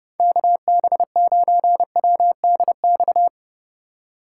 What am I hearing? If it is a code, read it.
Morse code, heard as a keyed tone, about 37 wpm, 700 Hz. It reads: KB9WDX
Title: Speed Racing - USA Call Signs 30wpm